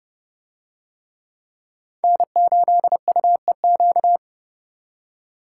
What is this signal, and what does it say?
Morse code, heard as a keyed tone, about 26 wpm, 700 Hz. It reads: N8UEQ